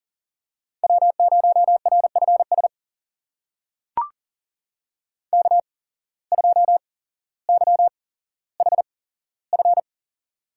Morse code, heard as a keyed tone, about 40 wpm, spaced as for 17 wpm, 700 Hz. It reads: W0RFS K2YHF
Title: USA Call Signs - 5 Characters - 8x Character Spacing - 40wpm